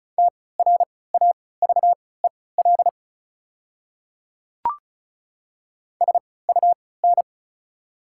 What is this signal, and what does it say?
Morse code, heard as a keyed tone, about 34 wpm, spaced as for 20 wpm, 690 Hz. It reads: TRAVEL SUN
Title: 3x Character Spacing - Top 500 Words - 35wpm